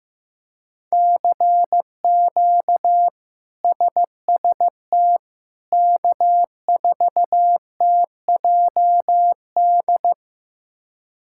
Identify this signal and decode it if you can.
Morse code, heard as a keyed tone, about 16 wpm, 700 Hz. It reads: CQ SST K4TJD
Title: K1USN SST QSO Practice - 15wpm